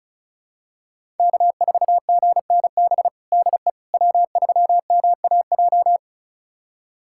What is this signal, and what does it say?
Morse code, heard as a keyed tone, about 34 wpm, 700 Hz. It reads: K4GNB DE W3MAJ
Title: USA Callsign DE Callsign - 5 Characters - 35wpm